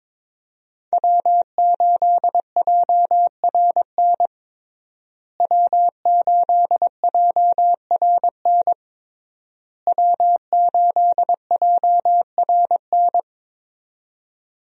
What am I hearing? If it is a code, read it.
Morse code, heard as a keyed tone, about 22 wpm, 700 Hz. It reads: W8JRN W8JRN W8JRN
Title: USA Call Signs - 5 Characters - Repeated 3x - 22wpm